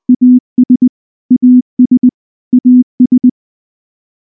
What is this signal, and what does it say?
Morse code, heard as a keyed tone, about 19 wpm, 250 Hz. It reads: AS AS AS
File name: morse / morse-training-20WPM-1